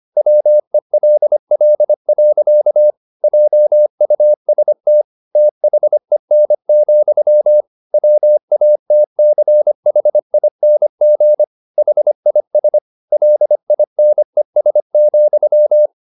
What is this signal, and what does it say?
Morse code, heard as a keyed tone, about 25 wpm, 600 Hz. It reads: WELL. JUST THEN, WATCHING HIS LINES,
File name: the old man and the sea part 1 25wpm